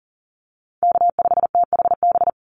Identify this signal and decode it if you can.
Morse code, heard as a keyed tone, about 40 wpm, 700 Hz. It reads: K5THB
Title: USA Call Signs 40wpm